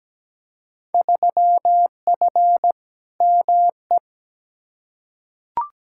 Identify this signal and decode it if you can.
Morse code, heard as a keyed tone, about 17 wpm, 700 Hz. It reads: 3F ME
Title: Field Day Exchanges - 17wpm